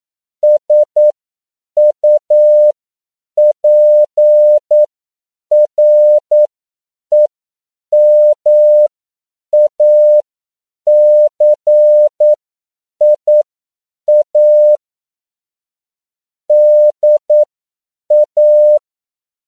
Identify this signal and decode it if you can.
Morse code, heard as a keyed tone, about 9 wpm, spaced as for 7 wpm, 600 Hz. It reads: SUPREMACIA DA